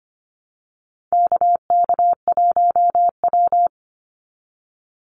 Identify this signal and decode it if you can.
Morse code, heard as a keyed tone, about 25 wpm, 700 Hz. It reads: KK1W
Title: NA SOTA Chaser Callsigns 25wpm